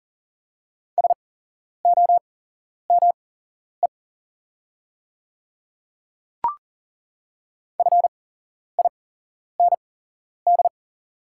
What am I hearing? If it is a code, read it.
Morse code, heard as a keyed tone, about 40 wpm, spaced as for 11 wpm, 700 Hz. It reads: SOME FIND
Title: 8x Character Spacing - Top 200 Words - 40wpm